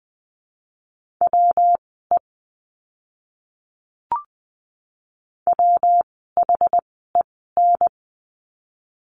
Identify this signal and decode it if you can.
Morse code, heard as a keyed tone, about 20 wpm, spaced as for 14 wpm, 700 Hz. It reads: WE WHEN
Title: 230 Lesson 37f - Letter x - Words 20wpm